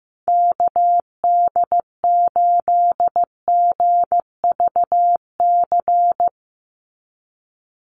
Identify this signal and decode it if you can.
Morse code, heard as a keyed tone, about 16 wpm, 700 Hz. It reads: KD8GVC
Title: USA Call Signs 15wpm